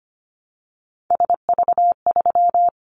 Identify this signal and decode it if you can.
Morse code, heard as a keyed tone, about 25 wpm, 700 Hz. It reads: SV3
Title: Callsign Prefixes for Top 50 DXCC - 25wpm